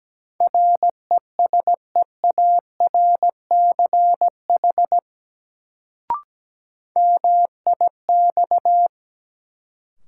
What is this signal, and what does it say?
Morse code, heard as a keyed tone, about 17 wpm, 700 Hz. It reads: RESEARCH MIX
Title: Sentences from Top 1000 Words - Review Part 4 - New Word Drill - 17wpm